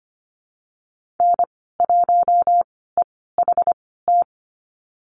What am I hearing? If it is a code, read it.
Morse code, heard as a keyed tone, about 25 wpm, spaced as for 16 wpm, 700 Hz. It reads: N1EHT
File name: Lesson 15g - Number 5 - Call Signs 25wpm